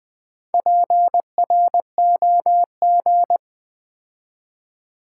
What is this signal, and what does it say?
Morse code, heard as a keyed tone, about 20 wpm, 700 Hz. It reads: PROG
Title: Build A Word - 4 Letters At A Time - 11-Letter Words - 20wpm